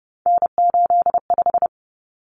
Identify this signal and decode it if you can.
Morse code, heard as a keyed tone, about 30 wpm, 700 Hz. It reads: N85
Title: Common Cut Numbers - 3-Digit 30wpm